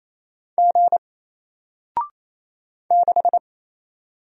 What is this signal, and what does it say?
Morse code, heard as a keyed tone, about 28 wpm, 700 Hz. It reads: G 6
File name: Single-Letter-Number-28wpm